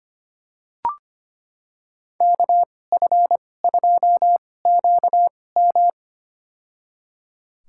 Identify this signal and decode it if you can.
Morse code, heard as a keyed tone, about 25 wpm, spaced as for 18 wpm, 700 Hz. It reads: KF2QM
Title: USA Call Signs - 5 Characters - 2x Character Spacing - 25wpm